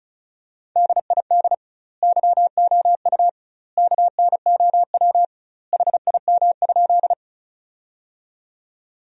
Morse code, heard as a keyed tone, about 36 wpm, 700 Hz. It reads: DID YOU KNOW HIM?
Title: 2x Word-Spacing - Sentences from Top 100 Words - 35wpm